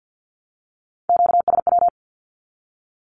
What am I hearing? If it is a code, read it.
Morse code, heard as a keyed tone, about 48 wpm, 700 Hz. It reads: QSW
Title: Q-Codes 50wpm